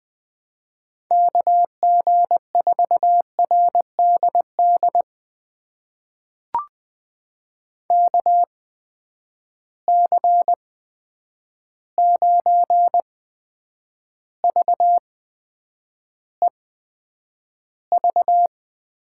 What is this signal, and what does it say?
Morse code, heard as a keyed tone, about 20 wpm, spaced as for 9 wpm, 700 Hz. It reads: KG4RDD KC9VEV
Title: USA Call Signs - 6 Characters - 8x Character Spacing - 20wpm